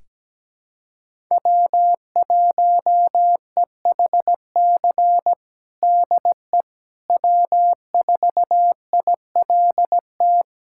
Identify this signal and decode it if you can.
Morse code, heard as a keyed tone, about 17 wpm, 700 Hz. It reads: W1EHC DE W4ILT
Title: USA Callsign DE Callsign - 5 Characters - 17wpm